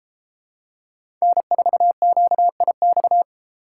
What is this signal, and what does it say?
Morse code, heard as a keyed tone, about 33 wpm, 700 Hz. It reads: N4QIX